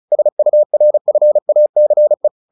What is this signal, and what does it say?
Morse code, heard as a keyed tone, about 34 wpm, 600 Hz. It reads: SURFACE
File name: the old man and the sea 2 35wpm